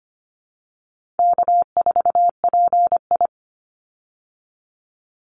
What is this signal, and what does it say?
Morse code, heard as a keyed tone, about 25 wpm, 700 Hz. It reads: K4PI
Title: Easy Call Signs Repeated 3x - 25wpm